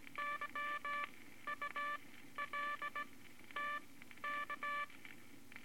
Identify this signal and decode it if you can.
Morse code, heard as a keyed tone, about 18 wpm, spaced as for 12 wpm, 1170 Hz, with noise. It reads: YULTK
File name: Morse Code Records / 18 B LCM 165 Ten Words Per Minute